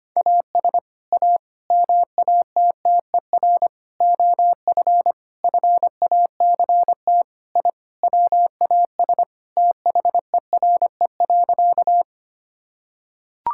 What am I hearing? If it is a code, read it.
Morse code, heard as a keyed tone, about 25 wpm, 700 Hz. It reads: AS A MATTER OF FACT I WAS THERE.
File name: practice / 25wpm / Sentences with 4-Word Phrases - 25wpm